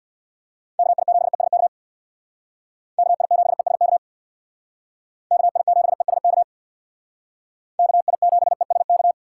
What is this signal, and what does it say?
Morse code, heard as a keyed tone, about 60 wpm, 700 Hz. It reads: KI7ESK KI7ESK KI7ESK KI7ESK
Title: Speed Racing - USA 2x3 Call Signs 50wpm